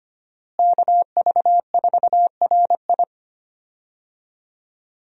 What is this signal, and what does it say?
Morse code, heard as a keyed tone, about 25 wpm, 700 Hz. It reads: KV4RI
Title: USA Call Signs - 5 Characters - Repeated 3x - 25wpm